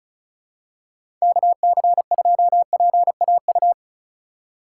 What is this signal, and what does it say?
Morse code, heard as a keyed tone, about 36 wpm, 700 Hz. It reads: KC2PAU
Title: USA Call Signs - 6 Characters - 3x Character Spacing - 35wpm